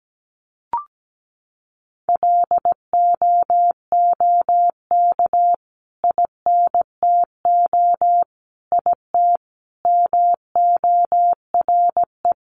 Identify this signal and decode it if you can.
Morse code, heard as a keyed tone, about 17 wpm, 700 Hz. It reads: LOOK INTO IT MORE
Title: Sentences from Top 1300 Words 17wpm